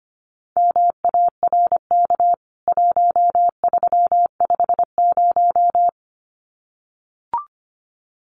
Morse code, heard as a keyed tone, about 25 wpm, 700 Hz. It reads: MARK 1350
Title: CWT Exchange 25wpm